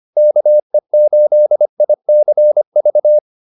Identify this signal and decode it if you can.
Morse code, heard as a keyed tone, about 25 wpm, 600 Hz. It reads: KE8ICV